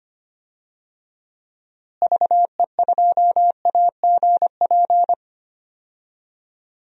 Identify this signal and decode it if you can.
Morse code, heard as a keyed tone, about 25 wpm, 700 Hz. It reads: VE2AGP
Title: DXCC-100 Call Signs - 6 Characters - Repeated 2x - 25wpm